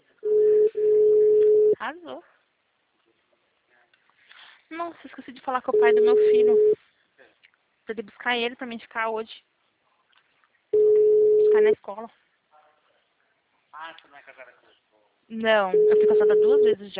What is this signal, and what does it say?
Ringback tone (Europe, 425 Hz), with speech.
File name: out- -2029- - - .56665